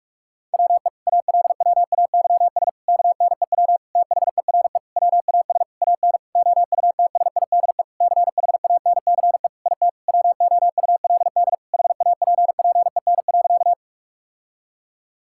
Morse code, heard as a keyed tone, about 46 wpm, 700 Hz. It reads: WE ALWAYS KNEW THERE WAS AN OUTSIDE CHANCE IT WOULD HAPPEN.
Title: Sentences from Top 1000 Words - Review Part 6 - 45wpm